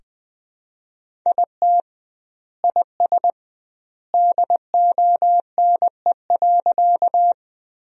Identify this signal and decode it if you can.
Morse code, heard as a keyed tone, about 20 wpm, 700 Hz. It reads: IT IS DONE.
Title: 2x Word-Spacing - Sentences from Top 500 Words - 20wpm